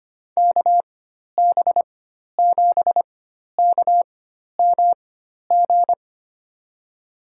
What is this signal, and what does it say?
Morse code, heard as a keyed tone, about 25 wpm, spaced as for 12 wpm, 700 Hz. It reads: KB7KMG
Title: USA Call Signs - 6 Characters - 4x Character Spacing - 25wpm